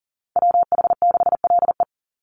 Morse code, heard as a keyed tone, about 40 wpm, 700 Hz. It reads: WH6LE
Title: NA SOTA Activator Callsigns 40wpm